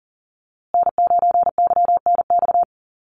Morse code, heard as a keyed tone, about 40 wpm, 700 Hz. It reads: N9YNX
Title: USA Call Signs Repeated 2x 40wpm